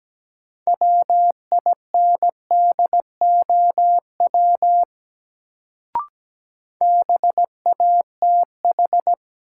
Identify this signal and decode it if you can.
Morse code, heard as a keyed tone, about 17 wpm, 700 Hz. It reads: WINDOW BATH
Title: Sentences from Top 1600 Words - New Word Drill - 17wpm